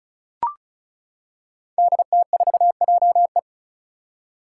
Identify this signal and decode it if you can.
Morse code, heard as a keyed tone, about 34 wpm, 700 Hz. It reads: DT4JE
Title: Five Challenging Letter-Number Combinations 35wpm